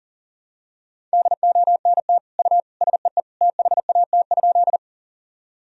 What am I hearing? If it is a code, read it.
Morse code, heard as a keyed tone, about 40 wpm, 700 Hz. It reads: DONT U SEE THAT?
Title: Sentences from Top 300 Words and Abbreviations 40wpm